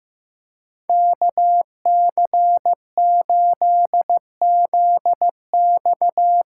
Morse code, heard as a keyed tone, about 15 wpm, 700 Hz. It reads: KC8ZX
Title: USA Call Signs - 6 Characters - 8x Character Spacing - 15wpm